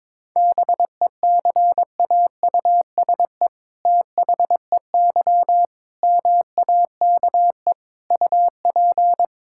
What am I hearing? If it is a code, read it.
Morse code, heard as a keyed tone, about 22 wpm, 700 Hz. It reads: BECAUSE THEY MAKE UP
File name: All-Age Appropriate Jokes - 22wpm